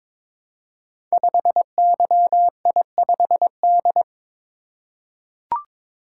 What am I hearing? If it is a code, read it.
Morse code, heard as a keyed tone, about 22 wpm, 700 Hz. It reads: 5YI5D
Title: Code Groups - 5 Letter:Numbers - 22wpm